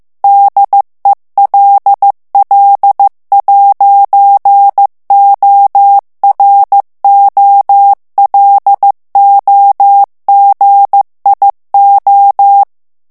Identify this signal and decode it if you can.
Morse code, heard as a keyed tone, about 15 wpm, 800 Hz. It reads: DELL'OROLOGIO
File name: morse mp3 / ITALIEN texte clair 1 vitesse 14 mots minute